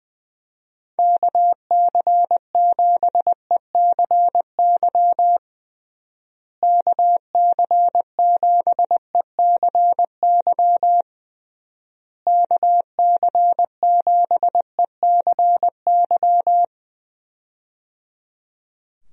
Morse code, heard as a keyed tone, about 20 wpm, 700 Hz. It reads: KC7ECY KC7ECY KC7ECY
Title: USA Call Signs - 6 Characters - Repeated 3x - 20wpm